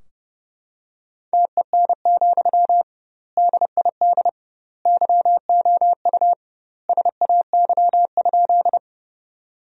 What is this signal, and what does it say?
Morse code, heard as a keyed tone, about 30 wpm, 700 Hz. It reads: TEN, DID YOU SAY?
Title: 2x Word-Spacing - Sentences from Top 400 Words - 30wpm